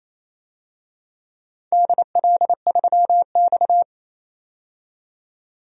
Morse code, heard as a keyed tone, about 27 wpm, 700 Hz. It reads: DL3X
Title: DXCC-100 Call Signs - 4 Characters - 28wpm